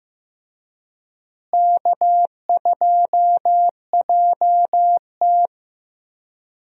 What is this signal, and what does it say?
Morse code, heard as a keyed tone, about 15 wpm, 700 Hz. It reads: K2JT
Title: USA Call Signs - 4 Characters - Repeated 3x - 15wpm